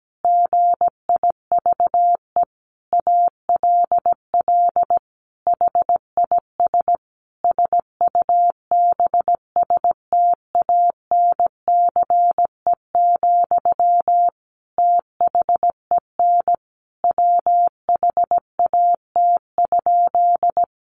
Morse code, heard as a keyed tone, about 17 wpm, 700 Hz. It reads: GIVE ALL HIS SUBSTANCE, THEN WHAT?
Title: Sentences from Top 1500 Words 17wpm